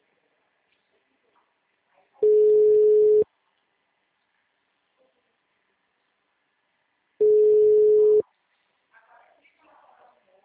Ringback tone (Europe, 425 Hz).